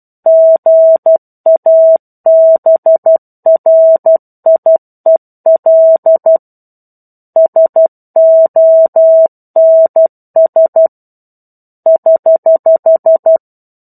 Morse code, heard as a keyed tone, about 12 wpm, 650 Hz. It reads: GABRIEL SONS <HH>
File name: 01-Chapter-01 12 WPM